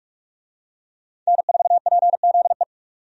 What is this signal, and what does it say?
Morse code, heard as a keyed tone, about 43 wpm, 700 Hz. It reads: N4PZE
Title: USA Call Signs - 5 Characters - 45wpm